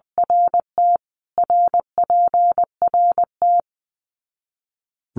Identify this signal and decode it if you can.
Morse code, heard as a keyed tone, about 20 wpm, 700 Hz. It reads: RT RPRT
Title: Abbreviations Repeated 3x - 20wpm